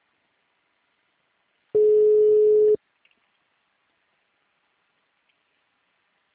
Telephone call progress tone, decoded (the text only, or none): ringback tone